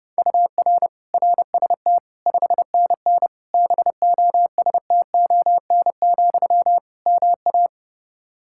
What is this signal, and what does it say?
Morse code, heard as a keyed tone, about 30 wpm, 700 Hz. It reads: UR RST 5NN BOSTON, MA
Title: Practice QSO - Exchange 1 - Beginning - 30wpm